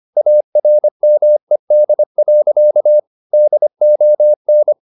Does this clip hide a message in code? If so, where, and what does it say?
Yes: Morse code from 0.2 to 4.7 s, heard as a keyed tone, about 25 wpm, 600 Hz. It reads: ARMED. DON